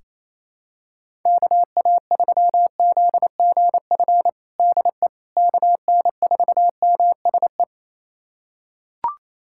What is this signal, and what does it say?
Morse code, heard as a keyed tone, about 28 wpm, 700 Hz. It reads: KA3ZGF DE KN4MSE
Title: USA Callsign DE Callsign 28wpm